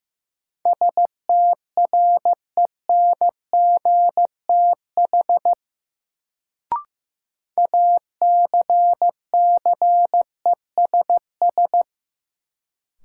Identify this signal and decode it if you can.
Morse code, heard as a keyed tone, about 15 wpm, 700 Hz. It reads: STRENGTH ACCESS
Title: Sentences from Top 1000 Words - Review Part 4 - New Word Drill - 15wpm